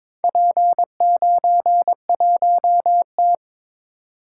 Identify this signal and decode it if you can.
Morse code, heard as a keyed tone, about 22 wpm, 700 Hz. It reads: P91T
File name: practice / 22wpm / Four Letter-Number Combinations - 22wpm